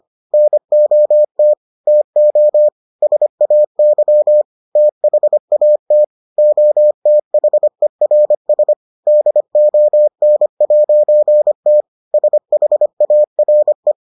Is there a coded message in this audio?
Yes: Morse code from 0.3 to 13.9 s, heard as a keyed tone, about 25 wpm, 600 Hz. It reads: NOT TO SAY THAT OTHERS DON'T SHARE